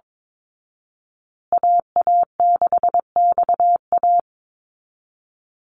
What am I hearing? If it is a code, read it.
Morse code, heard as a keyed tone, about 22 wpm, 700 Hz. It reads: AA6XA
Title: Call Signs Repeated 2x - 22wpm